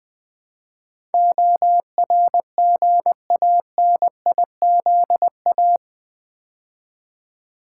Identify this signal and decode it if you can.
Morse code, heard as a keyed tone, about 21 wpm, 700 Hz. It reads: ORGANIZA
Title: Build A Word - 4 Letters At A Time - 12-Letter Words - 20wpm